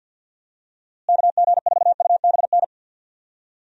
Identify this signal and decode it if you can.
Morse code, heard as a keyed tone, about 52 wpm, 700 Hz. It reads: KG4UBN
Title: USA Call Signs - 6 Characters - Repeated 3x - 50wpm